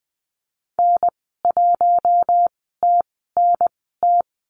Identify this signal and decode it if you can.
Morse code, heard as a keyed tone, about 20 wpm, spaced as for 15 wpm, 700 Hz. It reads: N1TNT
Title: 042 Lesson 8g - Number 1 - Call Signs 20wpm